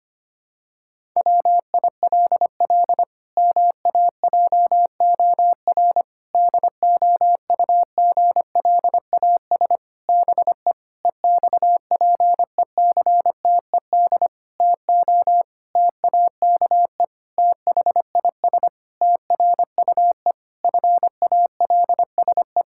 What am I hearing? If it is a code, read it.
Morse code, heard as a keyed tone, about 25 wpm, 700 Hz. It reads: WILL MAJOR DOUGLAS BE EXPECTED TO TAKE THIS TRUE FALSE
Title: Pangram Challenge v2 - 25wpm